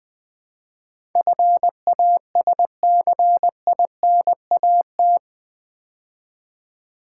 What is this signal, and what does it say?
Morse code, heard as a keyed tone, about 20 wpm, 700 Hz. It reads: FASCINAT
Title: Build A Word - 4 Letters At A Time - 11-Letter Words - 20wpm